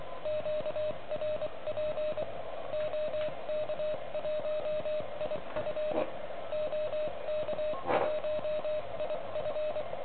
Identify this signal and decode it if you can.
Morse code, heard as a keyed tone, about 25 wpm, spaced as for 21 wpm, 610 Hz, with radio noise. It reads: QRP OK1IF OK1IF
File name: ARCHIV / OK QRP 2021 / REC10